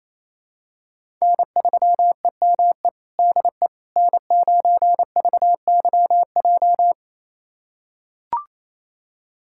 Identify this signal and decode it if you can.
Morse code, heard as a keyed tone, about 28 wpm, 700 Hz. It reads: N3EME DE N9VYJ
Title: USA Callsign DE Callsign - 5 Characters - 28wpm